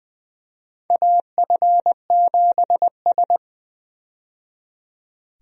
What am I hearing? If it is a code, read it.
Morse code, heard as a keyed tone, about 20 wpm, 700 Hz. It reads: AF7S
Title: USA Call Signs - 4 Characters - 20wpm